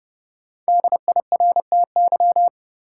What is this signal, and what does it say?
Morse code, heard as a keyed tone, about 30 wpm, 700 Hz. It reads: DIRTY
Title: Five Letter Words - 30wpm